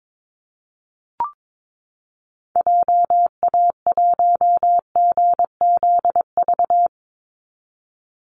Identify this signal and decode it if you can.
Morse code, heard as a keyed tone, about 22 wpm, 700 Hz. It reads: JA1GZV
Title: Call Signs Repeated 2x - 22wpm